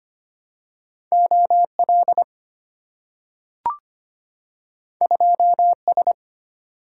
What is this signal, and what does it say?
Morse code, heard as a keyed tone, about 25 wpm, 700 Hz. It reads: OL 2S
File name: Two Letter-Number Combinations - 25wpm